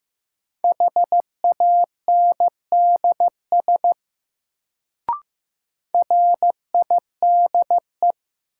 Morse code, heard as a keyed tone, about 15 wpm, 700 Hz. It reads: HANDS RIDE
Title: Sentences from Top 1000 Words - Review Part 6 - New Word Drill - 15wpm